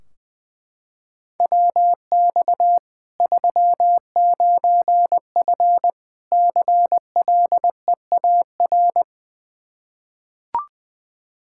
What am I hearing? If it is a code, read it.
Morse code, heard as a keyed tone, about 20 wpm, 700 Hz. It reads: WX 39F CLEAR
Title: Practice QSO - Exchange 2 - Info - 20wpm